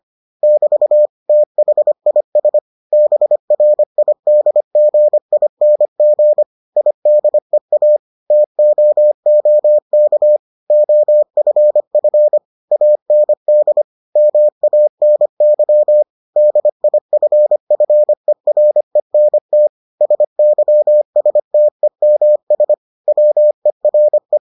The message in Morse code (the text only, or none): <BT> THIS BRIDGING IDEA TOOK OFF AND MANY DIFFERENT SYSTEMS WERE